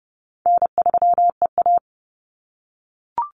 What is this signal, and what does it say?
Morse code, heard as a keyed tone, about 30 wpm, 700 Hz. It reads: N3EA
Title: Easy Call Signs - 30wpm